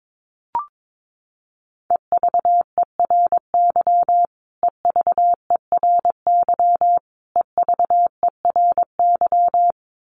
Morse code, heard as a keyed tone, about 22 wpm, 700 Hz. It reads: EVERY EVERY EVERY
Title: Top 200 Words Repeated 22wpm